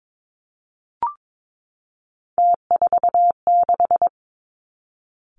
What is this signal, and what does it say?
Morse code, heard as a keyed tone, about 22 wpm, 700 Hz. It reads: T46
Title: Three Challenging Letter-Number Combinations 22wpm